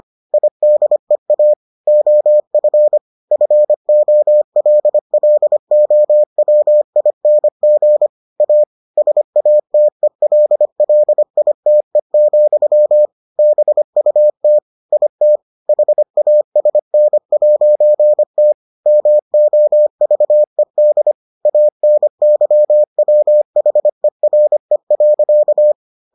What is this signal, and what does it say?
Morse code, heard as a keyed tone, about 25 wpm, 600 Hz. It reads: IDEA OF FOLLOWING A SATELLITE, BUT IT HASN'T MOVED ANYWHERE.